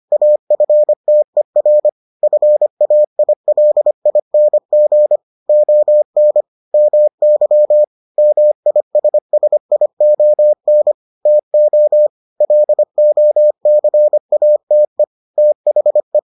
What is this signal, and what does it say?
Morse code, heard as a keyed tone, about 25 wpm, 600 Hz. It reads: AFTER FAILING ON MY MISSION TO LOCATE THE